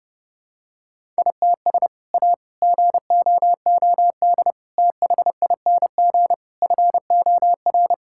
Morse code, heard as a keyed tone, about 30 wpm, 700 Hz. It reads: ITS A GOOD THING FOR
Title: Sentences from Top 1000 Words - Review Part 5 - 30wpm